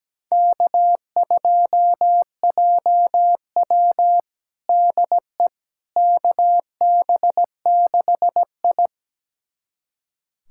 Morse code, heard as a keyed tone, about 17 wpm, 700 Hz. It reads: K2JW DE KB6I
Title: USA Callsign DE Callsign - 4 Characters - 17wpm